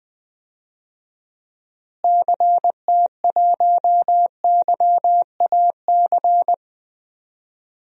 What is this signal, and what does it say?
Morse code, heard as a keyed tone, about 21 wpm, 700 Hz. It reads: CT1YAC
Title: DXCC-100 Call Signs - 6 Characters - Repeated 2x - 20wpm